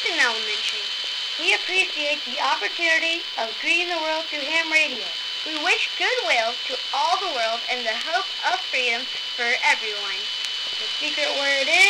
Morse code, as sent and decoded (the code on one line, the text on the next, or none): none